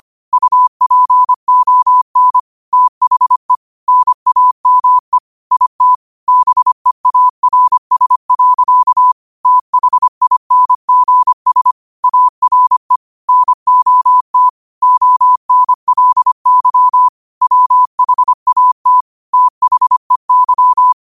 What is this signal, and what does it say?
Morse code, heard as a keyed tone, about 25 wpm, 1000 Hz. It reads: UPON THE NAME IT BEARS. THINGS ARE NOT ONLY WHAT THEY
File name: qotdfiles / 25wpm